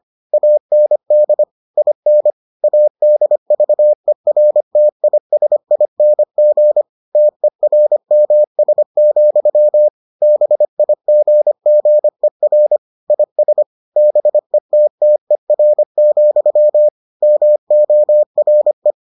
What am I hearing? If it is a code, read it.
Morse code, heard as a keyed tone, about 25 wpm, 600 Hz. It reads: AND IN ADVERTISING TERMS, BIGGER IS BETTER, MORE